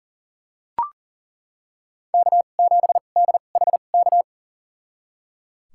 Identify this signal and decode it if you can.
Morse code, heard as a keyed tone, about 37 wpm, spaced as for 28 wpm, 700 Hz. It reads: K7DHK
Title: USA Call Signs - 5 Characters - 2x Character Spacing - 40wpm